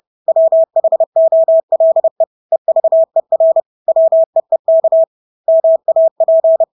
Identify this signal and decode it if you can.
Morse code, heard as a keyed tone, about 30 wpm, 650 Hz. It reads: WHOLE EVER WEEK MAP
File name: Engelske ord 30-30 wpm